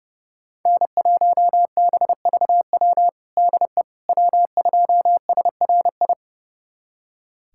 Morse code, heard as a keyed tone, about 30 wpm, 700 Hz. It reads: N1BVW DE W2SRI
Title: USA Callsign DE Callsign - 5 Characters - 30wpm